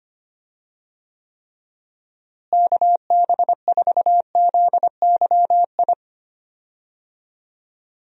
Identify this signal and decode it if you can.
Morse code, heard as a keyed tone, about 25 wpm, 700 Hz. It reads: KB4ZYI
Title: USA Call Signs - 6 Characters - Repeated 2x - 25wpm